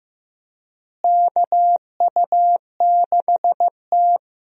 Morse code, heard as a keyed tone, about 16 wpm, 700 Hz. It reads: KU6T
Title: USA Call Signs - 4 Characters - Repeated 2x - 15wpm